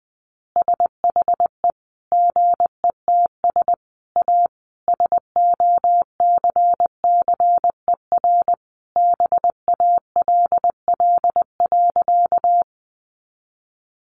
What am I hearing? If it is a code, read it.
Morse code, heard as a keyed tone, about 20 wpm, 700 Hz. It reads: SHE GETS A SOCCER BALL.
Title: Easy Phrases 20wpm